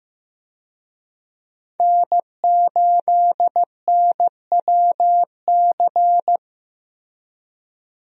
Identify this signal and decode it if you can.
Morse code, heard as a keyed tone, about 16 wpm, 700 Hz. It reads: N8NWC
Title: USA Call Signs - 5 Characters - Repeated 3x - 15wpm